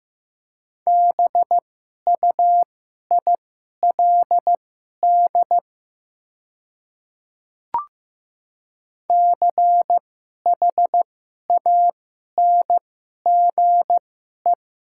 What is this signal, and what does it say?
Morse code, heard as a keyed tone, about 15 wpm, spaced as for 11 wpm, 700 Hz. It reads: BUILD CHANGE
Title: 2x Character Spacing - Top 200 Words - 15wpm